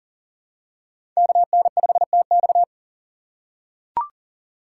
Morse code, heard as a keyed tone, about 40 wpm, 700 Hz. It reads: KN5TX
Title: K1USN SST QSO Practice - 40wpm